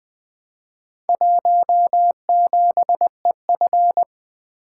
Morse code, heard as a keyed tone, about 20 wpm, 700 Hz. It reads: 17EF
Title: Four Letter-Number Combinations - 20wpm